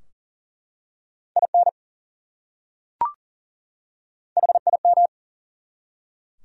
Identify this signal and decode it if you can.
Morse code, heard as a keyed tone, about 40 wpm, 700 Hz. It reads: IN HIM
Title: Top 100 Words 40wpm